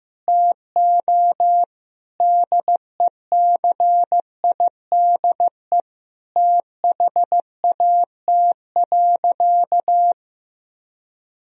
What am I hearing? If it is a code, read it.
Morse code, heard as a keyed tone, about 16 wpm, 700 Hz. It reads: TO DECIDE THAT.
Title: Sentences from Top 1000 Words - Review Part 3 - 15wpm